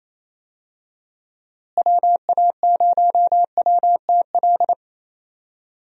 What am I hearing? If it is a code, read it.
Morse code, heard as a keyed tone, about 28 wpm, 700 Hz. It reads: WA0WTL
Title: USA Call Signs - 6 Characters - Repeated 2x - 28wpm